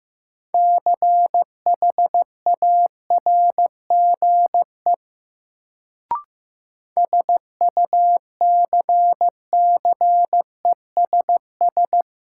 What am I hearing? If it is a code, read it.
Morse code, heard as a keyed tone, about 15 wpm, 700 Hz. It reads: CHARGE SUCCESS
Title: Sentences from Top 900 Words - New Word Drill - 15wpm